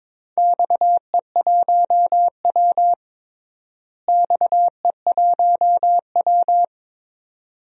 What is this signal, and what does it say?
Morse code, heard as a keyed tone, about 22 wpm, 700 Hz. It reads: XE1W XE1W
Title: DXCC-100 Call Signs - 4 Characters - Repeated 2x - 22wpm